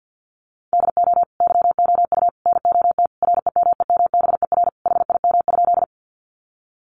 Morse code, heard as a keyed tone, about 50 wpm, 700 Hz. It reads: DO YOU NOT REMEMBER HIM?
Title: Sentences from Top 600 Words 50wpm